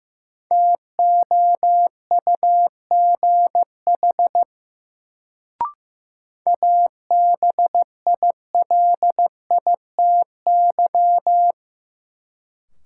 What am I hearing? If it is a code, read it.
Morse code, heard as a keyed tone, about 15 wpm, 700 Hz. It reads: TOUGH ABILITY
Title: Sentences from Top 1200 Words - New Word Drill Alt - 15wpm